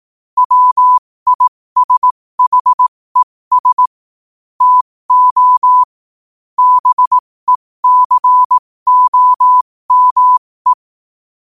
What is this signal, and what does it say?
Morse code, heard as a keyed tone, about 18 wpm, spaced as for 15 wpm, 1000 Hz. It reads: WISHES TO BECOME
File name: qotdfiles / 15wpm